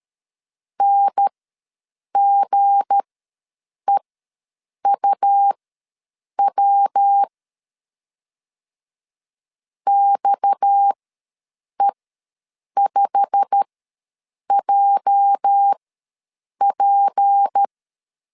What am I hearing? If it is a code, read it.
Morse code, heard as a keyed tone, about 13 wpm, spaced as for 7 wpm, 800 Hz. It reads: NGEUW XE5JP
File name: LEZIONI / 8 YJ45 / 8 YJ45 ws 12wpm 3146